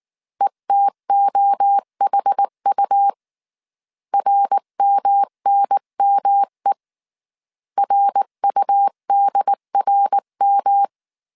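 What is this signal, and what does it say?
Morse code, heard as a keyed tone, about 19 wpm, 800 Hz. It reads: ETOHU RMNME RUDRM
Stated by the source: Morse code dictation at 18 wpm from the IW3SGT QRS Morse code training course, five-character groups of 300 characters.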